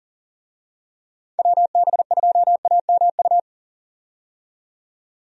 Morse code, heard as a keyed tone, about 40 wpm, 700 Hz. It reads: WB2AMU
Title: USA Call Signs - 6 Characters - 40wpm